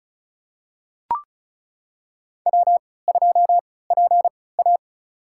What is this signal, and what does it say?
Morse code, heard as a keyed tone, about 34 wpm, spaced as for 20 wpm, 700 Hz. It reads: W2PA
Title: USA Call Signs - 4 Characters - 3x Character Spacing - 35wpm